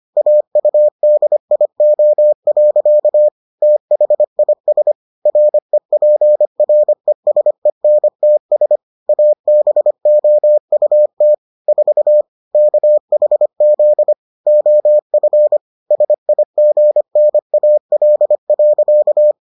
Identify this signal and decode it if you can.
Morse code, heard as a keyed tone, about 25 wpm, 600 Hz. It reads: AUDIO. THIS REPRESENTS ABOUT 4 KHZ OF SIGNAL.